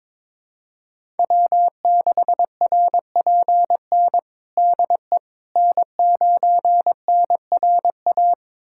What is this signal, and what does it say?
Morse code, heard as a keyed tone, about 22 wpm, 700 Hz. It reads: W6RPN DE N9NRA
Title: USA Callsign DE Callsign - 5 Characters - 22wpm